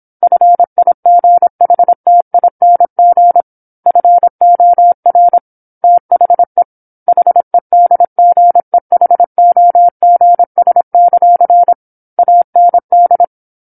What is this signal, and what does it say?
Morse code, heard as a keyed tone, about 27 wpm, 700 Hz. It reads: FIGHTING FOR THE HEDGEHOGS; AND
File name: 08-Chapter-08 26 WPM